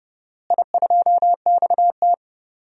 Morse code, heard as a keyed tone, about 30 wpm, 700 Hz. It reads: I2XT
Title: DXCC-100 Call Signs - 4 Characters - Repeated 2x - 30wpm